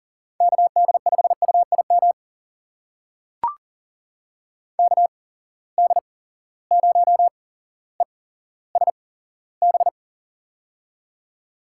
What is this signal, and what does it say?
Morse code, heard as a keyed tone, about 39 wpm, spaced as for 17 wpm, 700 Hz. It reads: KD5UIM KD0ESB
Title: USA Call Signs - 6 Characters - 8x Character Spacing - 40wpm